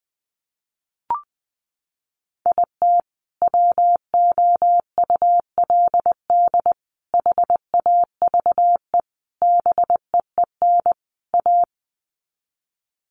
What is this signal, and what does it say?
Morse code, heard as a keyed tone, about 20 wpm, 700 Hz. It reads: IT WOULD HAVE BEEN A
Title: Sets of 5 Words Encore 20wpm